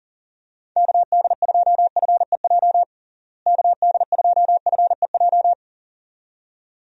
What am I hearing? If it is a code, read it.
Morse code, heard as a keyed tone, about 40 wpm, 700 Hz. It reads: KD2FEJ KD2FEJ
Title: USA Call Signs - 6 Characters - Repeated 2x - 40wpm